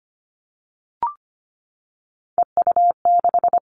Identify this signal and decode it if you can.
Morse code, heard as a keyed tone, about 25 wpm, 700 Hz. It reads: EU6
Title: All Cut Numbers - 3-Digit 25wpm